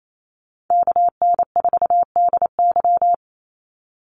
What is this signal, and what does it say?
Morse code, heard as a keyed tone, about 28 wpm, 700 Hz. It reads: KN4DY
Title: USA Call Signs 28wpm